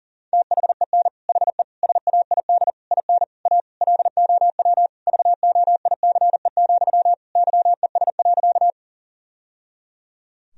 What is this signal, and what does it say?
Morse code, heard as a keyed tone, about 37 wpm, 700 Hz. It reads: THEN HE SAID IN A LOW VOICE, YES.
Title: Sentences from Top 1000 Words - Review Part 1 - 40wpm